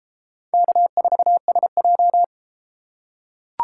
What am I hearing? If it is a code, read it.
Morse code, heard as a keyed tone, about 30 wpm, 700 Hz. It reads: K4SJ